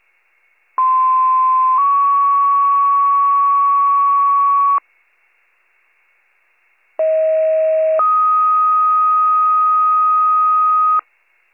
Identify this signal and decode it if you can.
Two-tone page, heard as a one-second tone, then a three-second tone, about 1040 then 1120 Hz, about 640 then 1250 Hz. It is heard from 0.8 to 11.0 s.